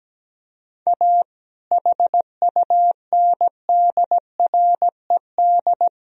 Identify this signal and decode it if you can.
Morse code, heard as a keyed tone, about 17 wpm, 700 Hz. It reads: A HUNDRED
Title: Sentences from Top 1000 Words - Review Part 2 - 17wpm